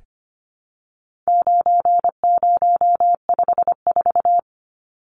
Morse code, heard as a keyed tone, about 25 wpm, 700 Hz. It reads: 9054
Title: Numbers - 4-Digit 25wpm